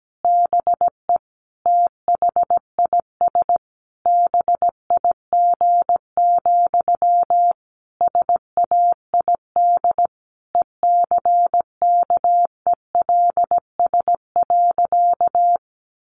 Morse code, heard as a keyed tone, about 17 wpm, 700 Hz. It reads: BE THIS BIG, SAID ECKELS.
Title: A Sound Of Thunder 17wpm